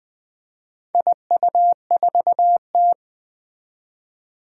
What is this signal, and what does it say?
Morse code, heard as a keyed tone, about 20 wpm, 700 Hz. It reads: IU4T
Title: Four Challenging Letter-Number Combinations 20wpm